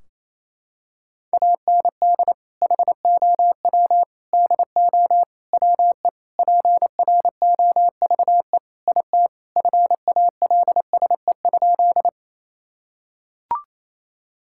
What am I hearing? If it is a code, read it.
Morse code, heard as a keyed tone, about 28 wpm, 700 Hz. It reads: AND HOW DO WE PROVE IT FALSE?
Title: Sentences from Top 1600 Words 28wpm